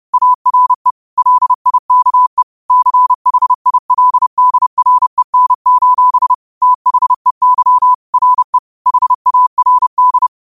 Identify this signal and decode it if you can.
Morse code, heard as a keyed tone, about 30 wpm, 1000 Hz. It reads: ARE LIKE CHILDREN: THEY RE HARD
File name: qotdfiles / 30wpm